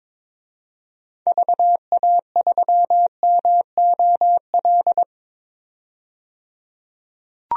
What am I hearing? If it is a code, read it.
Morse code, heard as a keyed tone, about 22 wpm, 700 Hz. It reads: VA3MOL
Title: DXCC-100 Call Signs - 6 Characters - 22wpm